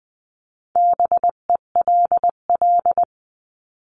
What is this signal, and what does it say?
Morse code, heard as a keyed tone, about 20 wpm, 700 Hz. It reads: BELL
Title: 259 Lesson 42d - Words 18-20wpm